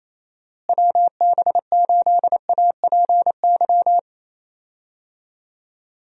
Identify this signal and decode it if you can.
Morse code, heard as a keyed tone, about 28 wpm, 700 Hz. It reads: WB8APY
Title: USA Call Signs - 6 Characters - Repeated 2x - 28wpm